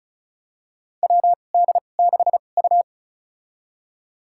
Morse code, heard as a keyed tone, about 34 wpm, spaced as for 25 wpm, 700 Hz. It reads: WD6U
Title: USA Call Signs - 4 Characters - 2x Character Spacing - 35wpm